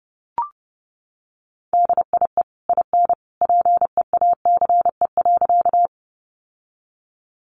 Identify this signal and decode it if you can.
Morse code, heard as a keyed tone, about 30 wpm, 700 Hz. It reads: DIE IN PEACE.
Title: Sentences from Top 1500 Words 30wpm